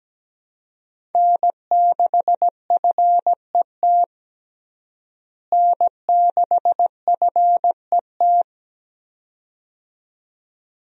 Morse code, heard as a keyed tone, about 17 wpm, 700 Hz. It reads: N6FET N6FET
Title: USA Call Signs - 5 Characters - Repeated 2x - 17wpm